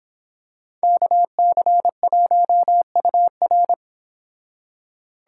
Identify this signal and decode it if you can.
Morse code, heard as a keyed tone, about 25 wpm, 700 Hz. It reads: KC1UR